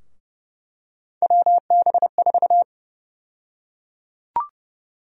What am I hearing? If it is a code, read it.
Morse code, heard as a keyed tone, about 30 wpm, 700 Hz. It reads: WB4
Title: USA Call Sign Prefixes - Repeated 2x - 30wpm